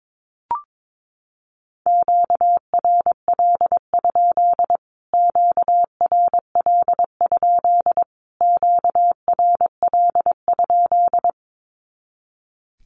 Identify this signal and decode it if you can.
Morse code, heard as a keyed tone, about 22 wpm, 700 Hz. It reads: QRL? QRL? QRL?
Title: Q-Codes Repeated 3x 22wpm